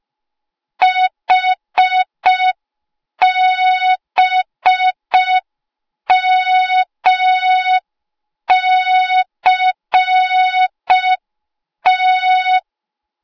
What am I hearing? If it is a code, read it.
Morse code, heard as a keyed tone, about 5 wpm, 3700 Hz. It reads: HBMCT